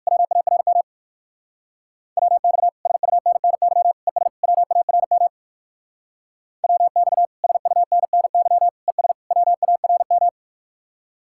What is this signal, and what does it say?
Morse code, heard as a keyed tone, about 58 wpm, 700 Hz. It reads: WARM WX SUNNY ES WARM WX SUNNY ES WARM